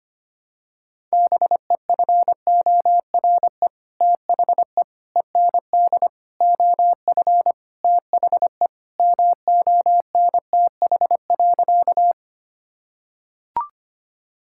Morse code, heard as a keyed tone, about 25 wpm, 700 Hz. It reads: BEFORE THE END OF THE MONTH.
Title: Sentences from Top 1000 Words - Review Part 5 - 25wpm